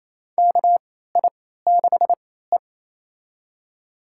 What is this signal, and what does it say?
Morse code, heard as a keyed tone, about 27 wpm, spaced as for 16 wpm, 700 Hz. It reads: KI6E
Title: USA Call Signs - 4 Characters - 3x Character Spacing - 28wpm